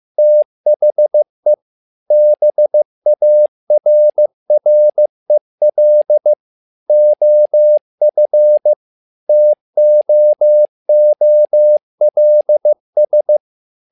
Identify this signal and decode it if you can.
Morse code, heard as a keyed tone, about 15 wpm, 600 Hz. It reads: THE BARREL OF TOOLS
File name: learning center / tower trouble / tower trouble 15wpm